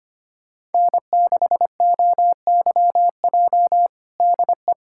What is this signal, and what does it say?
Morse code, heard as a keyed tone, about 25 wpm, 700 Hz. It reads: N6OYJ DE
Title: USA Callsign DE Callsign - 5 Characters - 25wpm